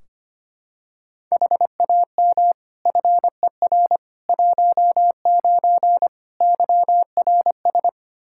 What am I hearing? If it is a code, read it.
Morse code, heard as a keyed tone, about 25 wpm, 700 Hz. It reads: HAM FER 19 YRS
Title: K1USN SST QSO Practice 25wpm